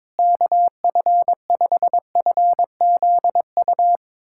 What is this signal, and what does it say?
Morse code, heard as a keyed tone, about 22 wpm, 700 Hz. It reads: KF5FZU
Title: USA Callsign DE Callsign 22wpm